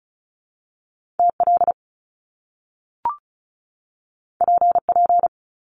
Morse code, heard as a keyed tone, about 36 wpm, 700 Hz. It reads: TL PP
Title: Top Two Letter Combinations in Words 35wpm